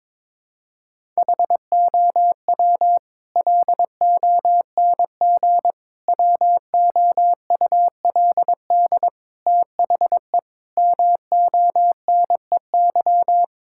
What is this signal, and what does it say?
Morse code, heard as a keyed tone, about 22 wpm, 700 Hz. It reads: HOW LONG WOULD THE MONEY
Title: Sentences from Top 1000 Words - Review Part 5 - 22wpm